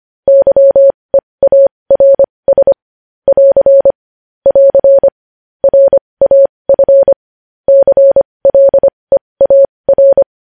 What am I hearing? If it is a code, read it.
Morse code, heard as a keyed tone, about 25 wpm, spaced as for 20 wpm, 550 Hz. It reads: YEARS <AR> <AR> RAF CLEAR
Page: News headlines in morse code at 20 wpm